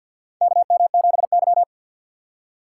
Morse code, heard as a keyed tone, about 52 wpm, 700 Hz. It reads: KM7Y
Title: USA Call Signs - 4x Character Spacing - 50wpm